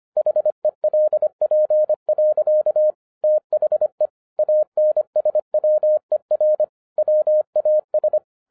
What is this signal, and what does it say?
Morse code, heard as a keyed tone, about 26 wpm, 610 Hz. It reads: HELP. THE ANSWER WAS
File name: 2012-01-mp3 / n14-FUN WITH NUMBERS-25wpm